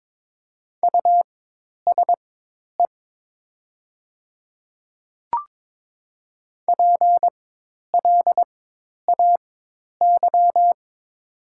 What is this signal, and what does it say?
Morse code, heard as a keyed tone, about 22 wpm, spaced as for 10 wpm, 700 Hz. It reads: USE PLAY
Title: 4x Character Spacing - Top 300 Words - 22wpm